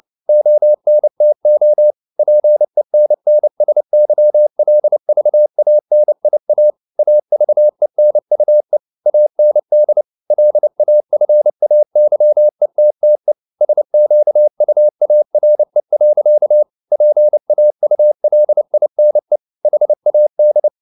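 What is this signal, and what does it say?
Morse code, heard as a keyed tone, about 29 wpm, 600 Hz. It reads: ONTO PENNSYLVANIA AVENUE AND LAFAYETTE SQUARE. PAULINE HAD
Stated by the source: Morse code at 29 wpm, sending Ken Follett's novel Never in English, from CW med Gnister.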